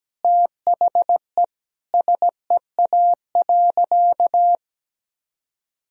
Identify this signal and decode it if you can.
Morse code, heard as a keyed tone, about 17 wpm, 700 Hz. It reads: THE SEA.
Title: Sentences from Top 1000 Words - Review Part 2 - 17wpm